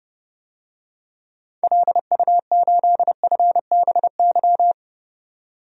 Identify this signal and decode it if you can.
Morse code, heard as a keyed tone, about 30 wpm, 700 Hz. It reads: LU8FBY
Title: DXCC-100 Call Signs - 6 Characters - 30wpm